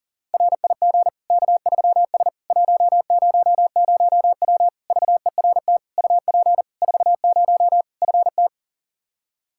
Morse code, heard as a keyed tone, about 40 wpm, 700 Hz. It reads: RIG K3S 100W VERT UP 40 FT
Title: K1USN SST QSO Practice 40wpm